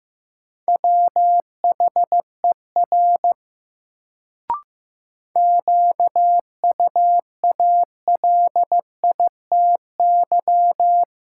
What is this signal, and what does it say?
Morse code, heard as a keyed tone, about 15 wpm, 700 Hz. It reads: WHER QUALITY
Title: Sentences from Top 1500 Words - New Word Drill Alt - 15wpm